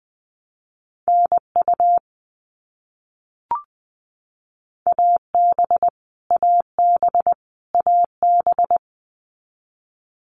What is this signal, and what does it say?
Morse code, heard as a keyed tone, about 20 wpm, 700 Hz. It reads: NU AB AB AB
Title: Canadian Provinces and Territories - Repeated 3x - 20wpm